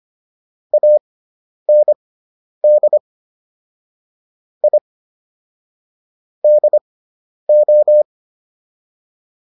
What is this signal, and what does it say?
Morse code, heard as a keyed tone, about 25 wpm, spaced as for 10 wpm, 600 Hz. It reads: AND I DO